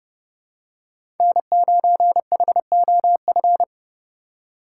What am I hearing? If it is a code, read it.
Morse code, heard as a keyed tone, about 30 wpm, 700 Hz. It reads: N9HOF
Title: USA Call Signs - 5 Characters - 2x Character Spacing - 30wpm